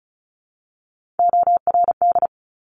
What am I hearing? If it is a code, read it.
Morse code, heard as a keyed tone, about 36 wpm, 700 Hz. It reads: ORD
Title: Top Three Letter Combinations in Words 35wpm